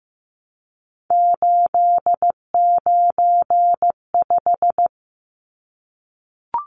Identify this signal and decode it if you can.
Morse code, heard as a keyed tone, about 15 wpm, 700 Hz. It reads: 895
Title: Numbers - 3-Digit 15wpm